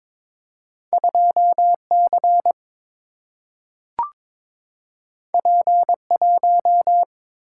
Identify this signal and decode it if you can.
Morse code, heard as a keyed tone, about 22 wpm, 700 Hz. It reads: 2 C P 1
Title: Single-Letter Single-Number 22wpm